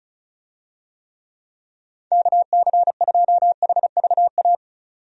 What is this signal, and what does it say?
Morse code, heard as a keyed tone, about 34 wpm, 700 Hz. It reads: KC2HVA
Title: USA Call Signs - 6 Characters - 35wpm